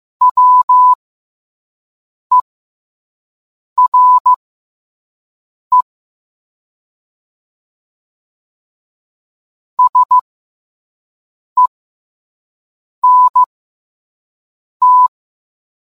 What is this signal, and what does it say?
Morse code, heard as a keyed tone, about 15 wpm, spaced as for 5 wpm, 1000 Hz. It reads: WERE SENT